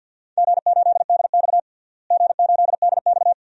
Morse code, heard as a keyed tone, about 47 wpm, 700 Hz. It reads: G8DX G8DX
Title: DXCC-100 Call Signs - 4 Characters - Repeated 2x - 50wpm